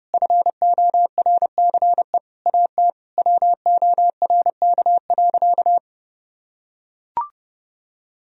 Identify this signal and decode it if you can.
Morse code, heard as a keyed tone, about 32 wpm, 700 Hz. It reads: FORCE AT WORK.
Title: Sentences from Top 1000 Words - Review Part 4 - 30wpm